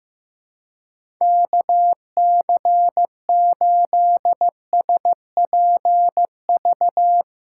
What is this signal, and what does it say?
Morse code, heard as a keyed tone, about 15 wpm, 700 Hz. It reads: KC8SPV
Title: USA Call Signs - 6 Characters - 3x Character Spacing - 15wpm